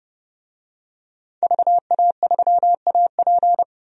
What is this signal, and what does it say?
Morse code, heard as a keyed tone, about 30 wpm, 700 Hz. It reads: VA3AP
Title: DXCC-100 Call Signs - 5 Characters - 30wpm